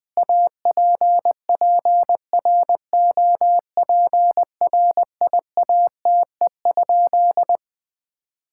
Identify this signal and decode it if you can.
Morse code, heard as a keyed tone, about 21 wpm, 700 Hz. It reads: APPROPRIATE?
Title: Sentences from Top 1000 Words - Review Part 5 - 20wpm